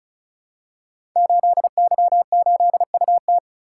Morse code, heard as a keyed tone, about 36 wpm, 700 Hz. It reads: 8Y8UT
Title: Code Groups - 5 Letter:Numbers - 35wpm